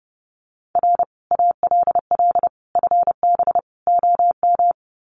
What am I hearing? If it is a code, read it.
Morse code, heard as a keyed tone, about 30 wpm, 700 Hz. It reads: R ALL FB OM
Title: Common QSO Phrases 30wpm